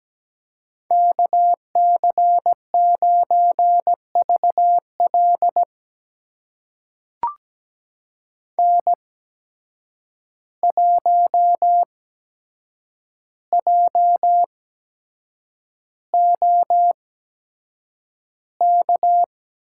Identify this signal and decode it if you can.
Morse code, heard as a keyed tone, about 17 wpm, spaced as for 7 wpm, 700 Hz. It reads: KC9VL N1JOK
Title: USA Call Signs - 5 Characters - 8x Character Spacing - 17wpm